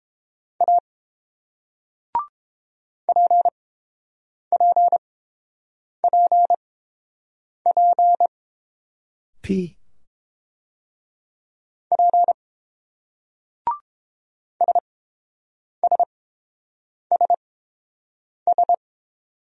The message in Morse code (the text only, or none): A PPPP P SSSS